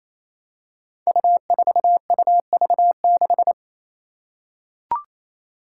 Morse code, heard as a keyed tone, about 27 wpm, 700 Hz. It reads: U4UV6
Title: Five Challenging Letter-Number Combinations 28wpm